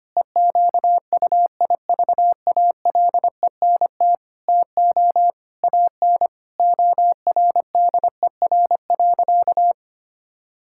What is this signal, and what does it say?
Morse code, heard as a keyed tone, about 25 wpm, 700 Hz. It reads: EQUIVALENT TO AN ORDER.
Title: Sentences from Top 1600 Words 25wpm